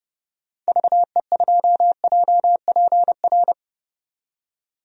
Morse code, heard as a keyed tone, about 31 wpm, 700 Hz. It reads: VE2JPR
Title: DXCC-100 Call Signs - 6 Characters - 30wpm